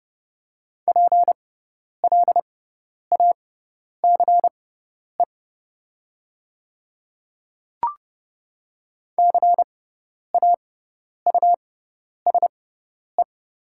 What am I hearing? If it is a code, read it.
Morse code, heard as a keyed tone, about 30 wpm, spaced as for 10 wpm, 700 Hz. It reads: PLACE CAUSE
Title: 6x Character Spacing - Top 300 Words - 30wpm